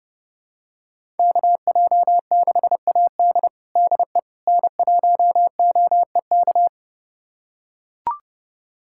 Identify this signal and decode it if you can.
Morse code, heard as a keyed tone, about 30 wpm, 700 Hz. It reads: KJ6AD DE N1OEK
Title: USA Callsign DE Callsign - 5 Characters - 30wpm